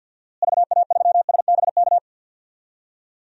Morse code, heard as a keyed tone, about 47 wpm, 700 Hz. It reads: VA3SBK
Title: DXCC-100 Call Signs - 6 Characters - Repeated 2x - 50wpm